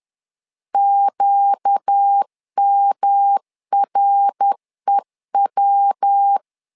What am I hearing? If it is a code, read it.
Morse code, heard as a keyed tone, about 10 wpm, 800 Hz. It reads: QMREW
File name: LEZIONI / 4 Q 10wpm 2210